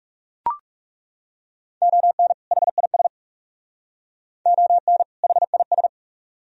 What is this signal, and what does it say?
Morse code, heard as a keyed tone, about 40 wpm, 700 Hz. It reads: ON HIS ON HIS